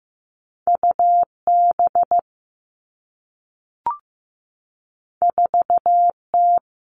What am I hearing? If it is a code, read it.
Morse code, heard as a keyed tone, about 15 wpm, 700 Hz. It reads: U B 4 T
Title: Two Challenging Letter-Number Combinations 15wpm